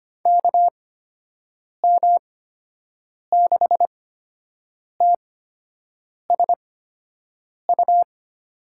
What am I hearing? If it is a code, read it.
Morse code, heard as a keyed tone, about 25 wpm, spaced as for 7 wpm, 700 Hz. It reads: KM6TSU
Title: USA Call Signs - 8x Character Spacing - 25wpm